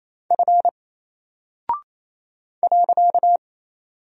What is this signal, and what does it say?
Morse code, heard as a keyed tone, about 27 wpm, 700 Hz. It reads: F .
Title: Single Character (Letter:Number:Punctuation:Prosign) 28wpm